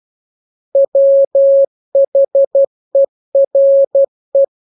Morse code, heard as a keyed tone, about 12 wpm, 550 Hz. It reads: WHERE
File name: jokes / joke 19 12 0000